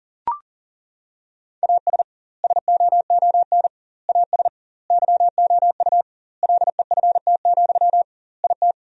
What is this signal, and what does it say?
Morse code, heard as a keyed tone, about 40 wpm, 700 Hz. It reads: AS SOON AS YOU LEFT, IT
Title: 2x Word-Spacing - Sentences from Top 400 Words - 40wpm